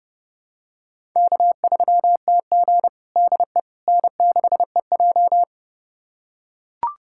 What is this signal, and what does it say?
Morse code, heard as a keyed tone, about 32 wpm, 700 Hz. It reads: K3TG DE N6EJ
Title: USA Callsign DE Callsign - 4 Characters - 30wpm